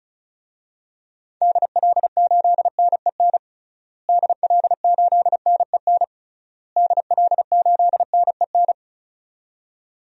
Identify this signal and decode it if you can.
Morse code, heard as a keyed tone, about 36 wpm, 700 Hz. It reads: DL8NEN DL8NEN DL8NEN
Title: DXCC-100 Call Signs - 6 Characters - Repeated 3x - 35wpm